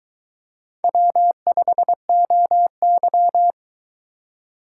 Morse code, heard as a keyed tone, about 21 wpm, 700 Hz. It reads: W5OY